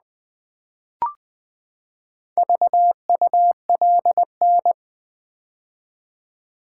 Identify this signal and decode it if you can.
Morse code, heard as a keyed tone, about 20 wpm, 700 Hz. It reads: VULN
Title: Build A Word - 4 Letters At A Time - 10-Letter Words - 20wpm